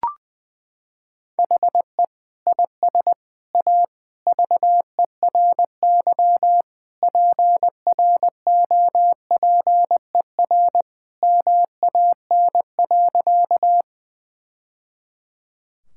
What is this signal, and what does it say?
Morse code, heard as a keyed tone, about 21 wpm, 700 Hz. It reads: HE IS A VERY PROPER MAN.
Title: Sentences from Top 1000 Words - Review Part 5 - 20wpm